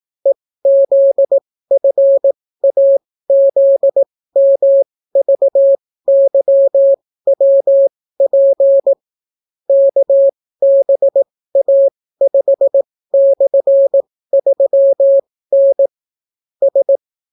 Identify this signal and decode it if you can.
Morse code, heard as a keyed tone, about 18 wpm, spaced as for 14 wpm, 550 Hz. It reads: EZFAZMVYWP KBA5/3N S